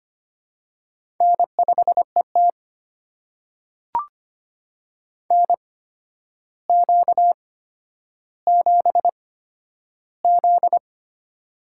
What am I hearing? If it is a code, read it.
Morse code, heard as a keyed tone, about 25 wpm, spaced as for 11 wpm, 700 Hz. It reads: N5ET NQ7Z
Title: USA Call Signs - 4 Characters - 8x Character Spacing - 25wpm